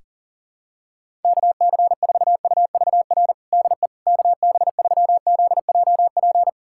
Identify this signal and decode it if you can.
Morse code, heard as a keyed tone, about 37 wpm, 700 Hz. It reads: KC4UVR DE KB3ZJP
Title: USA Callsign DE Callsign 40wpm